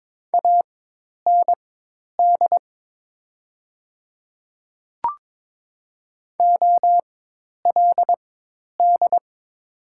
Morse code, heard as a keyed tone, about 22 wpm, spaced as for 10 wpm, 690 Hz. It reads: AND OLD
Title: 4x Character Spacing - Top 200 Words - 22wpm